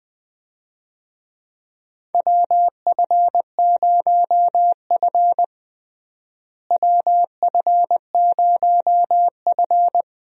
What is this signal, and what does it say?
Morse code, heard as a keyed tone, about 20 wpm, 700 Hz. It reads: WF0F WF0F
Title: USA Call Signs - 4 Characters - Repeated 2x - 20wpm